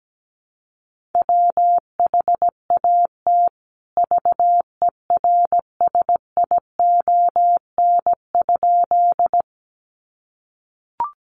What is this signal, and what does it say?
Morse code, heard as a keyed tone, about 17 wpm, 700 Hz. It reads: WHAT VERSION?
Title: Sentences from Top 1400 Words 17wpm